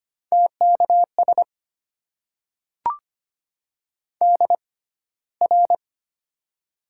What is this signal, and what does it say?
Morse code, heard as a keyed tone, about 25 wpm, spaced as for 15 wpm, 700 Hz. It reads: TKS DR
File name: Top 90 QSO Elements - 6x Character Spacing - 25wpm